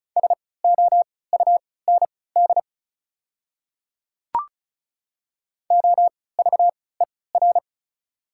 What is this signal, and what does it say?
Morse code, heard as a keyed tone, about 36 wpm, spaced as for 20 wpm, 690 Hz. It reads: SOUND OVER
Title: 3x Character Spacing - Top 100 Words - 35wpm